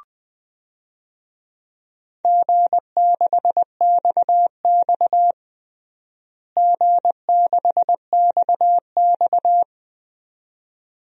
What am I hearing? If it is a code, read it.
Morse code, heard as a keyed tone, about 20 wpm, 700 Hz. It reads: G6XX G6XX
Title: DXCC-100 Call Signs - 4 Characters - Repeated 2x - 20wpm